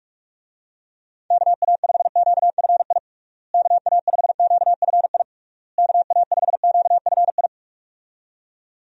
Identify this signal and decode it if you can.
Morse code, heard as a keyed tone, about 46 wpm, 700 Hz. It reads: KA5QFI KA5QFI KA5QFI
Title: USA Call Signs - 6 Characters - Repeated 3x - 45wpm